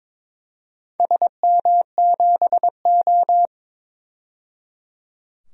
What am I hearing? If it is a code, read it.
Morse code, heard as a keyed tone, about 22 wpm, 700 Hz. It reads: SM7O
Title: DXCC-100 Call Signs - 4 Characters - 22wpm